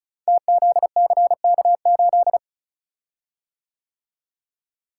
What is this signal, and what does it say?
Morse code, heard as a keyed tone, about 36 wpm, 700 Hz. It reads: TZCK8
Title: Code Groups - 5 Letter:Numbers - 35wpm